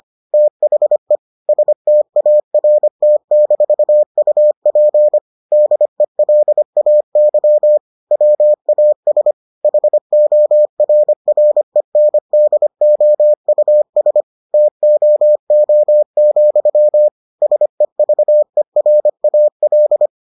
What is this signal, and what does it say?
Morse code, heard as a keyed tone, about 25 wpm, 600 Hz. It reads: THE START-UP DELAY WAS HORRENDOUS TOO, SEVERAL